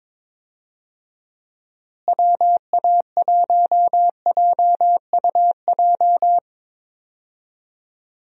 Morse code, heard as a keyed tone, about 21 wpm, 700 Hz. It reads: WA1JUJ
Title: USA Call Signs - 6 Characters - Repeated 3x - 22wpm